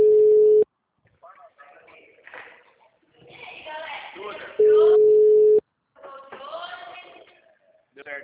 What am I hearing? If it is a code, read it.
Ringback tone (Europe, 425 Hz), with speech.